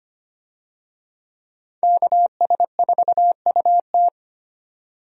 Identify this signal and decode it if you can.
Morse code, heard as a keyed tone, about 25 wpm, 700 Hz. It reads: KS4UT
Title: USA Call Signs - 5 Characters - 25wpm